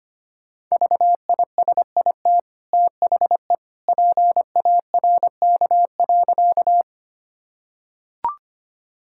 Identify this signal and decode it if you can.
Morse code, heard as a keyed tone, about 25 wpm, 700 Hz. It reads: VISIT THE PARK.
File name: Easy Sentences with Hints - 25wpm